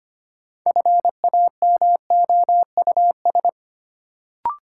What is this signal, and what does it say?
Morse code, heard as a keyed tone, about 25 wpm, 700 Hz. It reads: FAMOUS
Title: Sentences from Top 1500 Words - New Word Drill Alt - 25wpm